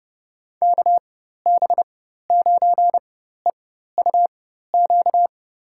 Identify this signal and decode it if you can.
Morse code, heard as a keyed tone, about 30 wpm, spaced as for 14 wpm, 700 Hz. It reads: KB9EUQ
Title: USA Call Signs - 4x Character Spacing - 30wpm